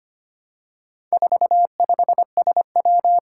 Morse code, heard as a keyed tone, about 25 wpm, 700 Hz. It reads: 45SW
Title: Four Challenging Letter-Number Combinations 25wpm